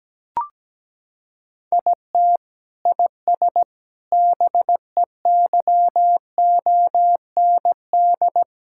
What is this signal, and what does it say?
Morse code, heard as a keyed tone, about 17 wpm, 700 Hz. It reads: IT IS BEYOND
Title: Sentences from Top 1000 Words - Review Part 6 - 17wpm